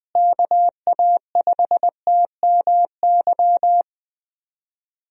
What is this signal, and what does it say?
Morse code, heard as a keyed tone, about 20 wpm, 700 Hz. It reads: KA5TMY
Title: USA Call Signs - 6 Characters - Repeated 3x - 20wpm